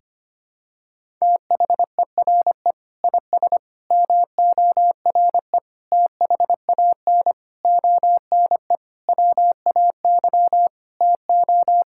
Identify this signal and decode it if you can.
Morse code, heard as a keyed tone, about 25 wpm, 700 Hz. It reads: THERE IS MORE THAN ONE WAY TO
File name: practice / 25wpm / Sentences with 9-Word Phrases - 25wpm